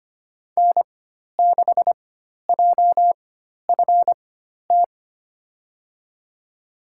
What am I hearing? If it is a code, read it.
Morse code, heard as a keyed tone, about 25 wpm, spaced as for 12 wpm, 700 Hz. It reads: N6JFT
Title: USA Call Signs - 5 Characters - 4x Character Spacing - 25wpm